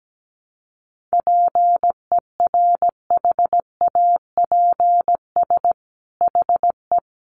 Morse code, heard as a keyed tone, about 17 wpm, 700 Hz. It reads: PERHAPS HE
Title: Sentences from Top 1300 Words 17wpm